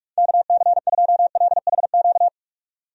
Morse code, heard as a keyed tone, about 41 wpm, 700 Hz. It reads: KC2LHQ
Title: USA Call Signs - 6 Characters - 45wpm